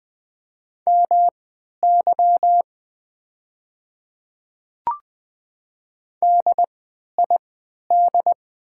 Morse code, heard as a keyed tone, about 20 wpm, spaced as for 11 wpm, 700 Hz. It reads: MY DID
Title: 3x Character Spacing - Top 200 Words - 20wpm